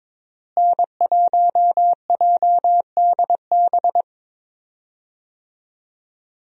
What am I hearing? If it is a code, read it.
Morse code, heard as a keyed tone, about 22 wpm, 700 Hz. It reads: N1JDB
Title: USA Call Signs - 5 Characters - Repeated 2x - 22wpm